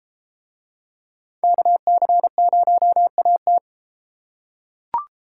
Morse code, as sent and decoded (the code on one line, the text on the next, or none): -.- -.-. ----- .- -
KC0AT